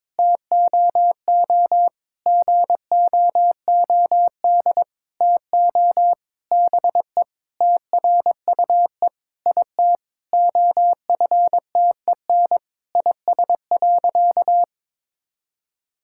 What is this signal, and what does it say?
Morse code, heard as a keyed tone, about 21 wpm, 700 Hz. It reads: TOO GOOD TO BE TRUE IT OFTEN IS.
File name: Sentences with 9-Word Phrases - 22wpm